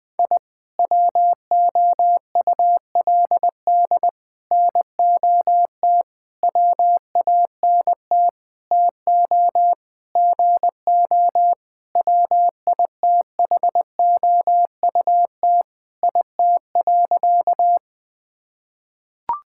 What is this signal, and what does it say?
Morse code, heard as a keyed tone, about 21 wpm, 700 Hz. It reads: I WOULD NOT WANT TO GO WITHOUT IT.
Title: Sentences with 5-Word Phrases - 20wpm